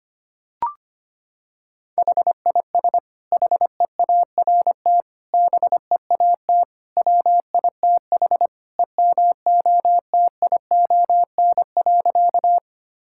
Morse code, heard as a keyed tone, about 25 wpm, 700 Hz. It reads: HIS HEART BEAT WITH EMOTION.
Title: Sentences from Top 1600 Words 25wpm